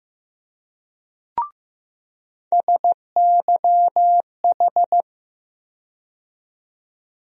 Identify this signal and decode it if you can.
Morse code, heard as a keyed tone, about 15 wpm, 700 Hz. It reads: SYH
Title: USA Call Sign Postfixes - 3 Letters 15wpm